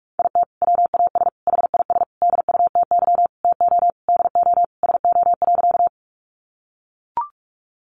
Morse code, heard as a keyed tone, about 44 wpm, 700 Hz. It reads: IT WAS HIS DUTY TO DO SO.
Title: Sentences from Top 1500 Words 45wpm